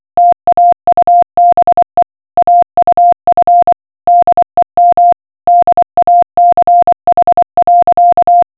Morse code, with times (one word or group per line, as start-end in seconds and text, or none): 0.2-2.0 TAUBE
2.4-3.7 AUF
4.1-5.1 DEM
5.5-8.4 DACH.